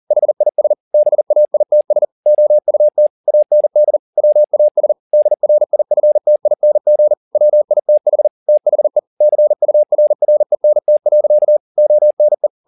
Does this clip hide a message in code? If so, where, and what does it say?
Yes: Morse code from 0.1 to 12.5 s, heard as a keyed tone, about 40 wpm, 600 Hz. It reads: HIS BAITS OUT AND WAS DRIFTING WITH THE CURRENT. ONE